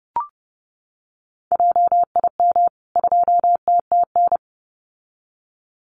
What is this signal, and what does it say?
Morse code, heard as a keyed tone, about 30 wpm, 700 Hz. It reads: JIM 2TTN
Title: CWT Exchange 30wpm